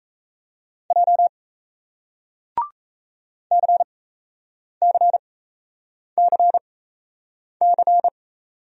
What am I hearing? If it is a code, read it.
Morse code, heard as a keyed tone, about 40 wpm, spaced as for 8 wpm, 700 Hz. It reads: J CCCC